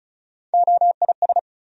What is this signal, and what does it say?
Morse code, heard as a keyed tone, about 34 wpm, 700 Hz. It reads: OIS
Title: USA Call Sign Postfixes - 3 Letters 35wpm